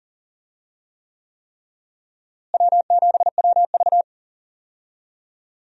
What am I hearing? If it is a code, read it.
Morse code, heard as a keyed tone, about 40 wpm, 700 Hz. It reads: W7WV
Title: USA Call Signs - 4 Characters - Repeated 3x - 40wpm